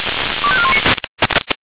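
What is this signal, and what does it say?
Five-tone selective call (ZVEI), heard as quick runs of five tones, tones of about 70 ms. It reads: 25419
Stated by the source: ZVEI five-tone selective call received with an SDR.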